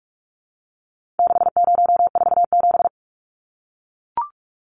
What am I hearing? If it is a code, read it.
Morse code, heard as a keyed tone, about 44 wpm, 700 Hz. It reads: 6047
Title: Numbers - 4-Digit 45wpm